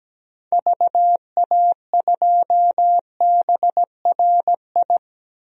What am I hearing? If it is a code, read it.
Morse code, heard as a keyed tone, about 17 wpm, 700 Hz. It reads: VA2BRI
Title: DXCC-100 Call Signs - 6 Characters - 17wpm